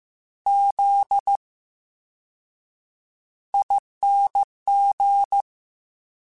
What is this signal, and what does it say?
Morse code, heard as a keyed tone, about 15 wpm, 800 Hz. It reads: Z I N G
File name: Special INGs-mp3s / Special INGs-mp3s / VerySlowToFaster Ing @13